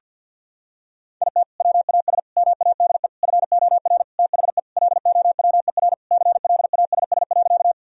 Morse code, heard as a keyed tone, about 50 wpm, 700 Hz. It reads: IT WAS MADE FOR THE LOWER CLASS.
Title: Sentences from Top 1000 Words - Review Part 4 - 50wpm